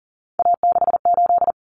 Morse code, heard as a keyed tone, about 40 wpm, 700 Hz. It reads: A68
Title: Common Cut Numbers - 3-Digit 40wpm